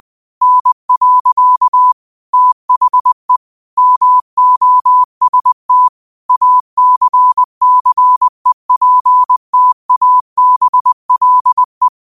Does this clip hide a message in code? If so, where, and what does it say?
Yes: Morse code from 0.4 to 11.9 s, heard as a keyed tone, about 20 wpm, 1000 Hz. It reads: N. THE MOST ACCEPTABLE